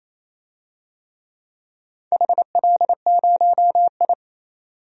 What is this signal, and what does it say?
Morse code, heard as a keyed tone, about 28 wpm, 700 Hz. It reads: HL0I
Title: DXCC-100 Call Signs - 4 Characters - 28wpm